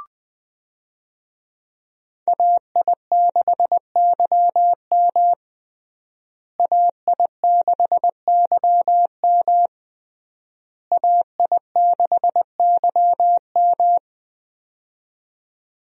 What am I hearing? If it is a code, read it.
Morse code, heard as a keyed tone, about 20 wpm, 700 Hz. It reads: AI6YM AI6YM AI6YM
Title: USA Call Signs - 5 Characters - Repeated 3x - 20wpm